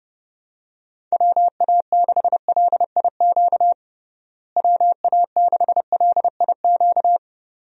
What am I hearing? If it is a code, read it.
Morse code, heard as a keyed tone, about 32 wpm, 700 Hz. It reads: WA6LIQ WA6LIQ
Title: USA Call Signs - 6 Characters - Repeated 2x - 30wpm